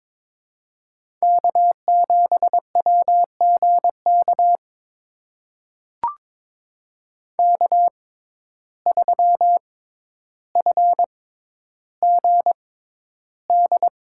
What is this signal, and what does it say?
Morse code, heard as a keyed tone, about 22 wpm, spaced as for 11 wpm, 700 Hz. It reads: K7WGK K3FGD
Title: USA Call Signs - 5 Characters - 6x Character Spacing - 22wpm